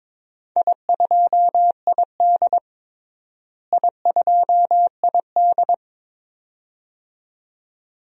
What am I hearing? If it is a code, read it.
Morse code, heard as a keyed tone, about 22 wpm, 700 Hz. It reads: I2ID I2ID
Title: DXCC-100 Call Signs - 4 Characters - Repeated 2x - 22wpm